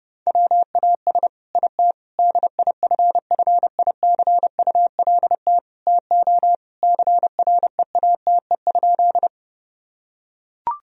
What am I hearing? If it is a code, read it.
Morse code, heard as a keyed tone, about 32 wpm, 700 Hz. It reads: WAS IT DIFFICULT TO CREATE?
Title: Sentences from Top 1000 Words - Review Part 3 - 30wpm